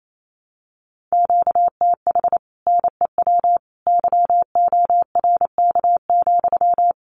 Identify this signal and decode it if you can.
Morse code, heard as a keyed tone, about 28 wpm, 700 Hz. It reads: QTH NEW YORK,
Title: Common QSO Phrases Encore 28wpm